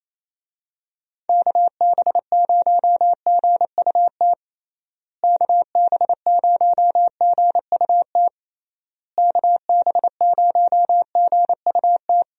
Morse code, heard as a keyed tone, about 27 wpm, 700 Hz. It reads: KB0GUT KB0GUT KB0GUT
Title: USA Call Signs - 6 Characters - Repeated 3x - 28wpm